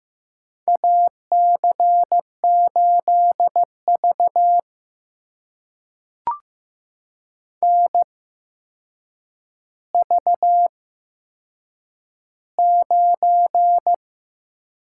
Morse code, heard as a keyed tone, about 15 wpm, spaced as for 7 wpm, 700 Hz. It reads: AC8V NV9
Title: USA Call Signs - 4 Characters - 8x Character Spacing - 15wpm